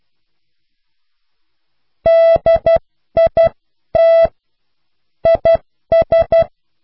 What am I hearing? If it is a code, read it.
Morse code, heard as a keyed tone, about 12 wpm, spaced as for 10 wpm, 660 Hz. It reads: DIT IS